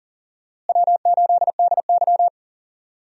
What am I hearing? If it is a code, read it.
Morse code, heard as a keyed tone, about 40 wpm, 700 Hz. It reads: W8DY
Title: USA Call Signs - 4 Characters - Repeated 2x - 40wpm